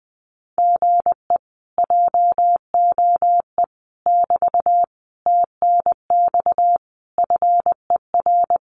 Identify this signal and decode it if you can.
Morse code, heard as a keyed tone, about 20 wpm, 700 Hz. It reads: GE JOE <BT> TNX FER
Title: Common QSO Phrases Encore 20wpm